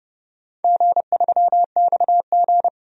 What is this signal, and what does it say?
Morse code, heard as a keyed tone, about 30 wpm, 700 Hz. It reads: G3XG
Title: DXCC-100 Call Signs - 4 Characters - 30wpm